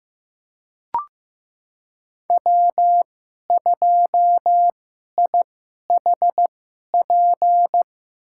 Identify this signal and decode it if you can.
Morse code, heard as a keyed tone, about 15 wpm, spaced as for 11 wpm, 700 Hz. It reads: W2IHP
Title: USA Call Signs - 5 Characters - 2x Character Spacing - 15wpm